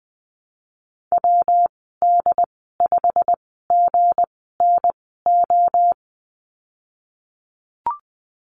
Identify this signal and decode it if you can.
Morse code, heard as a keyed tone, about 20 wpm, spaced as for 15 wpm, 700 Hz. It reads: WD5GNO
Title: 179 Lesson 29g - Slash - Call Signs 20wpm